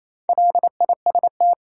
Morse code, heard as a keyed tone, about 28 wpm, 700 Hz. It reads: LIST
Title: Four Letter Words - 28wpm